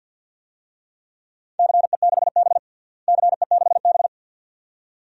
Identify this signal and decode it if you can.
Morse code, heard as a keyed tone, about 52 wpm, 700 Hz. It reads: CE6B CE6B
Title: DXCC-100 Call Signs - 4 Characters - Repeated 2x - 50wpm